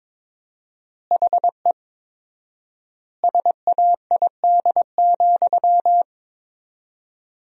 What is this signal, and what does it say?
Morse code, heard as a keyed tone, about 22 wpm, 700 Hz. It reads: HE SAID,
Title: 4x Word-Spacing - Sentences from Top 500 Words - 22wpm